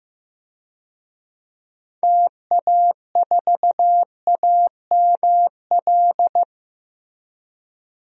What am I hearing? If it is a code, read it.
Morse code, heard as a keyed tone, about 15 wpm, 700 Hz. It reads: TA4AML
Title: DXCC-100 Call Signs - 6 Characters - Repeated 3x - 15wpm